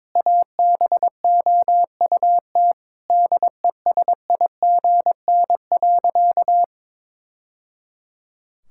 Morse code, heard as a keyed tone, about 22 wpm, 700 Hz. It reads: ABOUT DESIGN.
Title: Sentences from Top 1000 Words - Review Part 5 - 22wpm